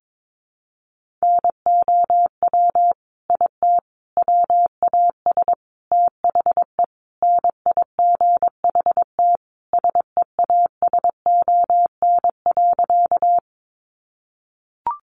Morse code, heard as a keyed tone, about 22 wpm, 700 Hz. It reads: NOW IT WAS THE NIGHT SEASON.
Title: Sentences from Top 1400 Words 22wpm